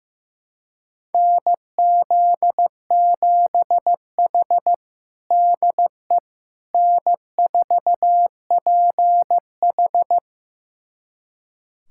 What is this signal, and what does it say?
Morse code, heard as a keyed tone, about 15 wpm, 700 Hz. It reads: NZ7H DE N4PH
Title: USA Callsign DE Callsign - 4 Characters - 15wpm